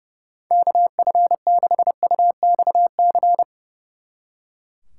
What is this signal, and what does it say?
Morse code, heard as a keyed tone, about 36 wpm, 700 Hz. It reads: KF6UXC
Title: Speed Racing - USA Call Signs 30wpm